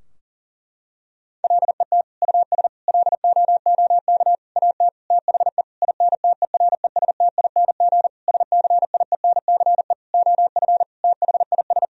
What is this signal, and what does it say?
Morse code, heard as a keyed tone, about 40 wpm, 700 Hz. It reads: LET US LOOK AT THE INTERESTING SCIENCE OF THIS
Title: Sentences from Top 1000 Words - Review Part 2 - 40wpm